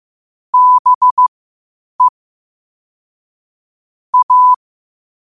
Morse code, heard as a keyed tone, about 15 wpm, spaced as for 8 wpm, 1000 Hz. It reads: BE A
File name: qotdfiles / 8wpm